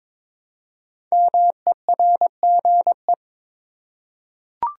Morse code, heard as a keyed tone, about 22 wpm, 700 Hz. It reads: MERGE
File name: 5 Letter Words - First 3 Letters Given - 22wpm